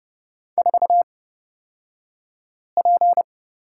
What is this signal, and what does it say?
Morse code, heard as a keyed tone, about 30 wpm, spaced as for 5 wpm, 700 Hz. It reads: 4P
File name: Lesson 38d - Letter q - Two Characters 30wpm